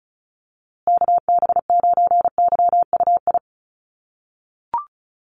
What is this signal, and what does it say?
Morse code, heard as a keyed tone, about 34 wpm, 700 Hz. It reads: KB9YUI
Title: USA Call Signs Repeated 3x 35wpm